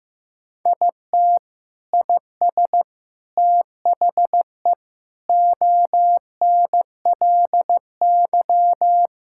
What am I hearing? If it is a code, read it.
Morse code, heard as a keyed tone, about 15 wpm, 700 Hz. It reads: IT IS THE ONLY
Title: Sentences from Top 1000 Words - Review Part 5 - 15wpm